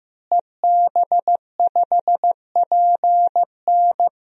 Morse code, heard as a keyed tone, about 15 wpm, 700 Hz. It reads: EB5PN
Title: DXCC-100 Call Signs - 5 Characters - Repeated 2x - 15wpm